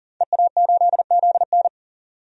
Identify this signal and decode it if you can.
Morse code, heard as a keyed tone, about 40 wpm, 700 Hz. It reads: EA8ZN
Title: DXCC-100 Call Signs - 5 Characters - 40wpm